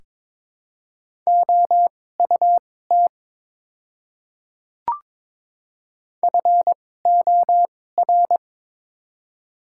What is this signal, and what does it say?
Morse code, heard as a keyed tone, about 22 wpm, spaced as for 16 wpm, 700 Hz. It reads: OUT FOR
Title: 2x Character Spacing - Top 100 Words - 22wpm